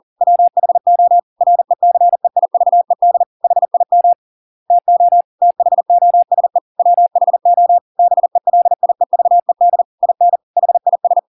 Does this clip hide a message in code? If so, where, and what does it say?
Yes: Morse code from 0.2 to 11.2 s, heard as a keyed tone, about 40 wpm, 700 Hz. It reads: WHO RECEIVED HIM TO THOSE WHO BELIEVED IN HIS